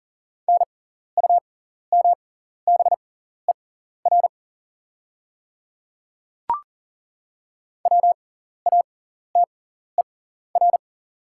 Morse code, heard as a keyed tone, about 40 wpm, spaced as for 14 wpm, 700 Hz. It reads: NUMBER WATER
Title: 6x Character Spacing - Top 100 Words - 40wpm